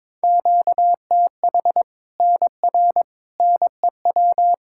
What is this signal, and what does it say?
Morse code, heard as a keyed tone, about 22 wpm, 700 Hz. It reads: QTH NR NEW
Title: K1USN SST QSO Practice 22wpm